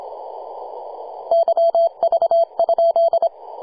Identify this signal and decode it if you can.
Morse code, heard as a keyed tone, about 25 wpm, 670 Hz. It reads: YV?
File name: posts / 2015-sept-26-2040z-v73a-21-025-cw